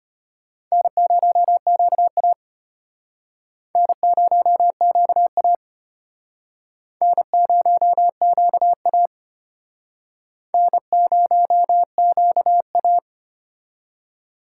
Morse code, heard as a keyed tone, about 34 wpm, 700 Hz. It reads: N0QA N0QA N0QA N0QA